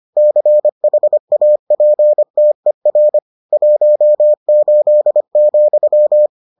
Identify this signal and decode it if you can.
Morse code, heard as a keyed tone, about 25 wpm, 600 Hz. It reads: CHAPTER 18,